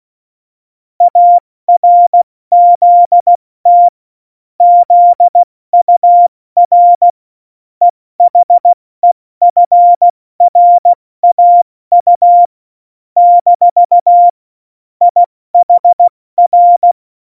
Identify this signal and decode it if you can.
Morse code, heard as a keyed tone, about 16 wpm, 700 Hz. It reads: ARZT ZUR EHEFRAU - IHR
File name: Morsekurs / Witze / Witz 04. Februar 2020, 70 F80